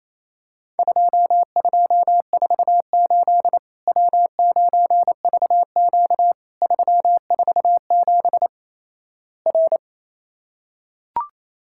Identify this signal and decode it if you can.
Morse code, heard as a keyed tone, about 28 wpm, 700 Hz. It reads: 2248 W9VQ 347 R
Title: Worked All Europe DX Contest - QTC Exchange - 28wpm